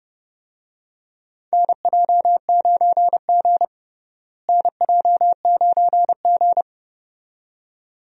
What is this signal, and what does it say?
Morse code, heard as a keyed tone, about 31 wpm, 700 Hz. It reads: NJ9G NJ9G
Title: USA Call Signs - 4 Characters - Repeated 2x - 30wpm